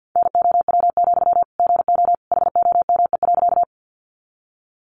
Morse code, heard as a keyed tone, about 50 wpm, 700 Hz. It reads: NOW, GO HOME.
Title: Sentences from Top 200 Words 50wpm